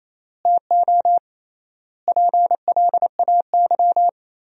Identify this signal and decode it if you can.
Morse code, heard as a keyed tone, about 28 wpm, 700 Hz. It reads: TO PLAY
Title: 3x Word-Spacing - Sentences from Top 300 Words - 28wpm